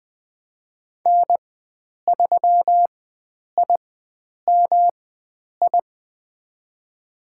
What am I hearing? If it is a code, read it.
Morse code, heard as a keyed tone, about 20 wpm, spaced as for 9 wpm, 700 Hz. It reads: N3IMI
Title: USA Call Signs - 5 Characters - 4x Character Spacing - 20wpm